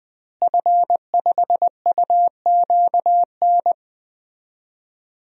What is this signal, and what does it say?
Morse code, heard as a keyed tone, about 20 wpm, 700 Hz. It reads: F5UQN
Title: DXCC-100 Call Signs - 5 Characters - Repeated 3x - 20wpm